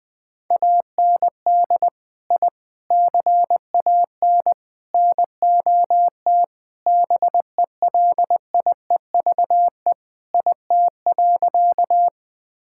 Morse code, heard as a keyed tone, about 20 wpm, 700 Hz. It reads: AND I CAN NOT BELIEVE IT.
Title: Sentences with 4-Word Phrases - 20wpm